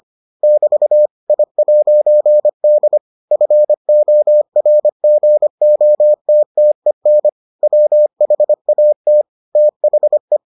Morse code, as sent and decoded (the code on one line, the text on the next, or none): -...- / .. .----. -.. / ..-. --- .-. --. --- - - . -. / .-- .... .- - / - .... .
<BT> I'D FORGOTTEN WHAT THE